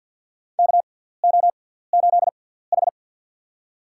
Morse code, heard as a keyed tone, about 48 wpm, spaced as for 17 wpm, 700 Hz. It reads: KO8H
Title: USA Call Signs - 4 Characters - 6x Character Spacing - 50wpm